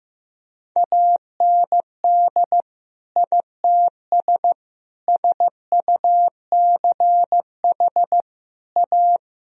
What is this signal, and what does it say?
Morse code, heard as a keyed tone, about 14 wpm, 700 Hz. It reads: AND ITS SUCH A
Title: Sentences from Top 1000 Words - Review Part 2 - 15wpm